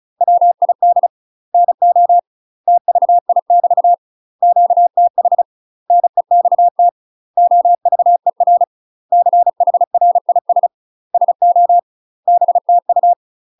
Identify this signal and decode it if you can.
Morse code, heard as a keyed tone, about 34 wpm, 700 Hz. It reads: WID NO TVI <BT> QTH NEXT OVER CHRIS SO BTU
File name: UK MASTER QSOs TRAINING / 35WPM / UK MASTER QSOs 0018